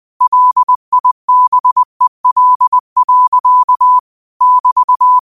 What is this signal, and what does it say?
Morse code, heard as a keyed tone, about 20 wpm, 1000 Hz. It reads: LIBEL. <BT>
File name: qotdfiles / 20wpm